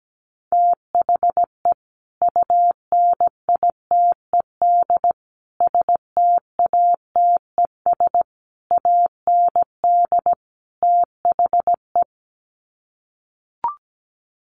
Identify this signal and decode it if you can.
Morse code, heard as a keyed tone, about 17 wpm, 700 Hz. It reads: THE UNITED STATES AND THE
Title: Sets of 5 Words 17wpm